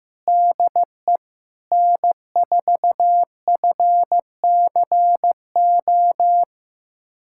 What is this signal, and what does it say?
Morse code, heard as a keyed tone, about 15 wpm, 700 Hz. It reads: DE N4FCO
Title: USA Callsign DE Callsign - 5 Characters - 15wpm